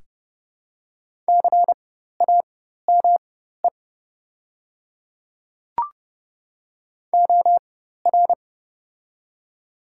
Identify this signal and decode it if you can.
Morse code, heard as a keyed tone, about 32 wpm, spaced as for 14 wpm, 700 Hz. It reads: CAME OR
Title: 4x Character Spacing - Top 500 Words - 30wpm